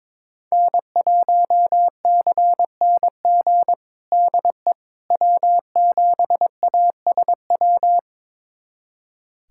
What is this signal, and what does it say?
Morse code, heard as a keyed tone, about 22 wpm, 700 Hz. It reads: N1CNG DE W7ASW
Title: USA Callsign DE Callsign - 5 Characters - 22wpm